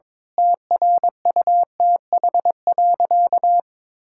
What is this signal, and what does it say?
Morse code, heard as a keyed tone, about 22 wpm, 700 Hz. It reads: TRUTH.
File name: Sentences with 9-Word Phrases - 22wpm